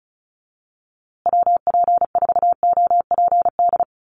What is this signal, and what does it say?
Morse code, heard as a keyed tone, about 34 wpm, 700 Hz. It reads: WP4OPD
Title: USA Call Signs Repeated 2x 35wpm